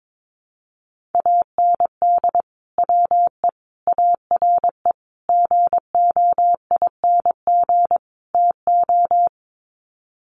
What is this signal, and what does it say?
Morse code, heard as a keyed tone, about 22 wpm, 700 Hz. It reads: AND WE ARE GOING TO
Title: Sets of 5 Words Encore2 22wpm